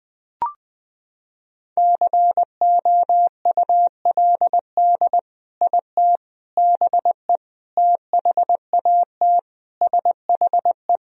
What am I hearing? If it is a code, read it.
Morse code, heard as a keyed tone, about 20 wpm, 700 Hz. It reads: COULD IT BE THAT SHE
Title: Sentences from Top 1000 Words - Review Part 1 - 20wpm